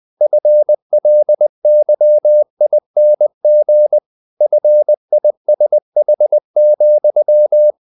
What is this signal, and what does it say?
Morse code, heard as a keyed tone, about 20 wpm, 600 Hz. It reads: FLYING FISH,
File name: the old man and the sea part 2 20wpm